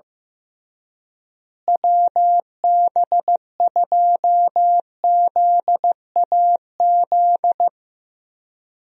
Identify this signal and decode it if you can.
Morse code, heard as a keyed tone, about 15 wpm, 700 Hz. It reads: WB2ZAZ
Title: USA Call Signs - 6 Characters - Repeated 3x - 15wpm